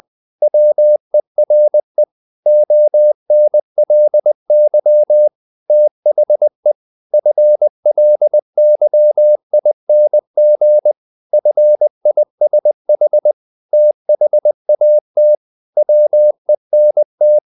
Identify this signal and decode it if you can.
Morse code, heard as a keyed tone, about 20 wpm, 600 Hz. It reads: WERE ONLY THE FLYING FISH THAT WENT
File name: the old man and the sea part 2 20wpm